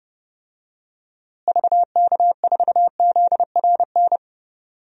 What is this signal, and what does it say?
Morse code, heard as a keyed tone, about 32 wpm, 700 Hz. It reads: VK4ZRN
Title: DXCC-100 Call Signs - 6 Characters - Repeated 2x - 30wpm